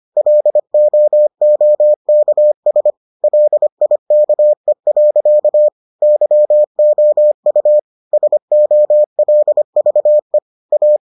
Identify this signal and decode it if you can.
Morse code, heard as a keyed tone, about 25 wpm, 600 Hz. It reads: LOOKS LIKE. YOU SOLVE A